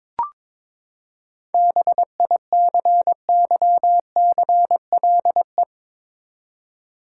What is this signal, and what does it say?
Morse code, heard as a keyed tone, about 22 wpm, 700 Hz. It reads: BICYCLE
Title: Seven Letter Words - 22wpm